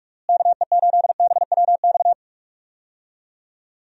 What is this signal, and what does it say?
Morse code, heard as a keyed tone, about 46 wpm, 700 Hz. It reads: KE8BWX
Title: USA Call Signs - 6 Characters - Repeated 3x - 45wpm